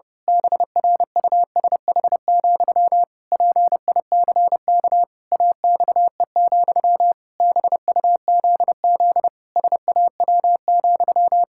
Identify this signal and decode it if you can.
Morse code, heard as a keyed tone, about 30 wpm, 700 Hz. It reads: BRUSH, PICK AXE, BUZZ SAW,
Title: Pangram Challenge v2 - 30wpm